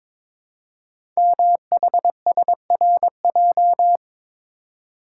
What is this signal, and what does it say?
Morse code, heard as a keyed tone, about 22 wpm, 700 Hz. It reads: MHSRJ
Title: Code Groups - 5 Letters - 22wpm